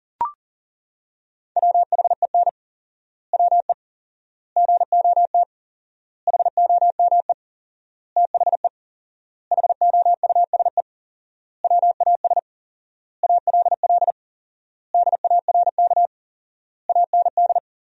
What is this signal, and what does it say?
Morse code, heard as a keyed tone, about 40 wpm, 700 Hz. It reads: WHEN WE GOT HOME THE HOUSE WAS ALL DARK AND
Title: 4x Word-Spacing - Sentences from Top 500 Words - 40wpm